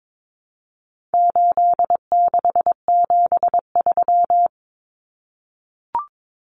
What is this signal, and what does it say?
Morse code, heard as a keyed tone, about 22 wpm, 700 Hz. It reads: 8673
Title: Common Cut Numbers - 4-Digit 22wpm